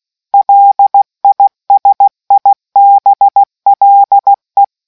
Morse code, heard as a keyed tone, about 16 wpm, 800 Hz. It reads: LISIBLE